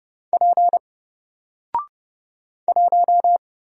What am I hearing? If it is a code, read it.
Morse code, heard as a keyed tone, about 30 wpm, 700 Hz. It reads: P 1
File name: Single-Letter-Number-30wpm